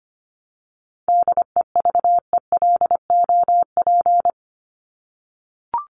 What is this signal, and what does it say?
Morse code, heard as a keyed tone, about 25 wpm, 700 Hz. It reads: DEVELOP
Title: ED Suffix - Top 100 Words with Base Words 25wpm